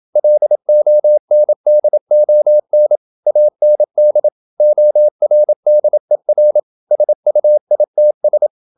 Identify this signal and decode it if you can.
Morse code, heard as a keyed tone, about 27 wpm, 600 Hz. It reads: LONDON AND ORDER SUITS
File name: Never 0027wpm